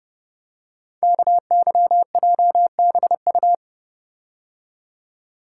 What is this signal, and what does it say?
Morse code, heard as a keyed tone, about 30 wpm, 700 Hz. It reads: KYJBU
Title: Code Groups - 5 Letters - 30wpm